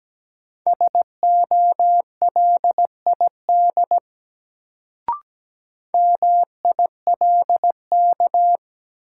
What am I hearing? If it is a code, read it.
Morse code, heard as a keyed tone, about 17 wpm, 700 Hz. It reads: SOLID MILK
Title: Sentences from Top 1500 Words - New Word Drill - 17wpm